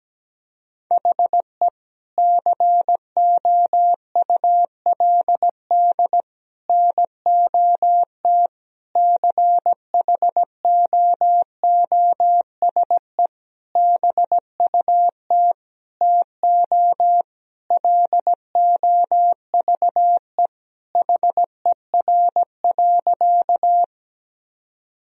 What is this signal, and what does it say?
Morse code, heard as a keyed tone, about 17 wpm, 700 Hz. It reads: HE COULD NOT CHOOSE BUT TO LOVE HER.
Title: Sentences from Top 1000 Words - Review Part 4 - 17wpm